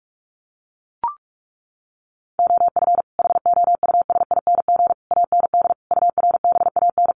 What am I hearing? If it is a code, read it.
Morse code, heard as a keyed tone, about 44 wpm, 700 Hz. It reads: OF HOUSING AND URBAN
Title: Sets of 5 Words Encore2 45wpm